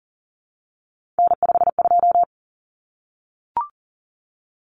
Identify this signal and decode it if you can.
Morse code, heard as a keyed tone, about 40 wpm, 700 Hz. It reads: N52
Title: Common Cut Numbers - 3-Digit 40wpm